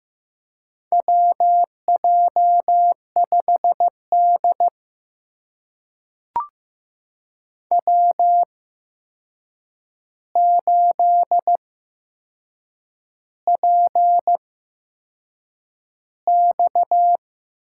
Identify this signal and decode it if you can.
Morse code, heard as a keyed tone, about 15 wpm, spaced as for 6 wpm, 700 Hz. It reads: WJ5D W8PX
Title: USA Call Signs - 4 Characters - 8x Character Spacing - 15wpm